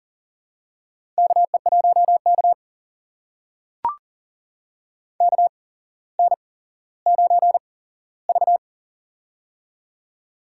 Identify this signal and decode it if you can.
Morse code, heard as a keyed tone, about 38 wpm, spaced as for 17 wpm, 700 Hz. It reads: KE1K KN9V
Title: USA Call Signs - 4 Characters - 8x Character Spacing - 40wpm